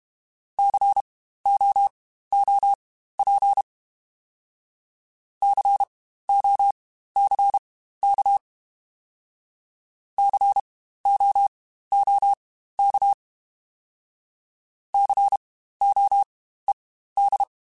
Morse code, heard as a keyed tone, about 32 wpm, spaced as for 15 wpm, 800 Hz. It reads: COOP COCK COOK COED
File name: Spaced Alphas / Spaced Alphas mp3s / Cs sp @30